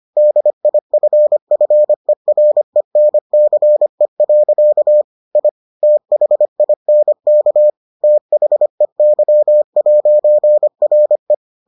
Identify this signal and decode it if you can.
Morse code, heard as a keyed tone, about 25 wpm, 600 Hz. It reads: DIFFERENCE. I THINK THEY'RE